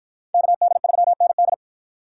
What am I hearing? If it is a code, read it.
Morse code, heard as a keyed tone, about 51 wpm, 700 Hz. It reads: KD3ND